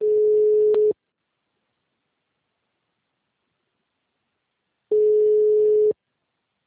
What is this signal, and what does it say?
Ringback tone (Europe, 425 Hz), with speech.
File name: out- -2013- - - .0